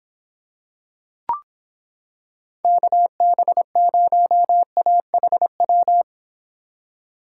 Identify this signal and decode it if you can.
Morse code, heard as a keyed tone, about 25 wpm, 700 Hz. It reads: KB0AHW